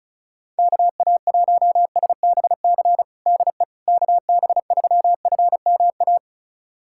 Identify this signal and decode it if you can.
Morse code, heard as a keyed tone, about 34 wpm, 700 Hz. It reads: KA1SBC DE KB3FMA
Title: USA Callsign DE Callsign - 6 Characters - 35wpm